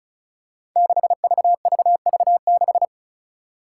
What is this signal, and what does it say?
Morse code, heard as a keyed tone, about 34 wpm, 700 Hz. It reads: 6VVV6
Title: Five Challenging Letter-Number Combinations 35wpm